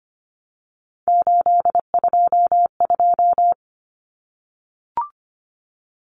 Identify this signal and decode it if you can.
Morse code, heard as a keyed tone, about 25 wpm, 700 Hz. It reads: 822
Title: Common Cut Numbers - 3-Digit 25wpm